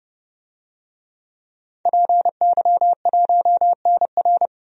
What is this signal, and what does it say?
Morse code, heard as a keyed tone, about 29 wpm, 700 Hz. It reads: PY1NR
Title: DXCC-100 Call Signs - 5 Characters - 30wpm